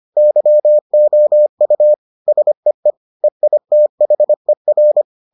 Morse code, heard as a keyed tone, about 25 wpm, 600 Hz. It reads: YOU SEE EITHER